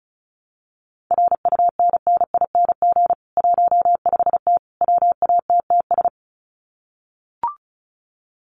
Morse code, heard as a keyed tone, about 34 wpm, 700 Hz. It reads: RUNNING 15T WATTS
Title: Common QSO Phrases 35wpm